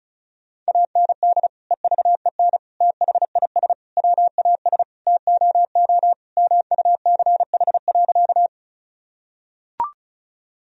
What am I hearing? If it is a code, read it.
Morse code, heard as a keyed tone, about 36 wpm, 700 Hz. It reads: AND EVEN THIS WAS TOO MUCH.
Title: Sentences from Top 1000 Words - Review Part 1 - 35wpm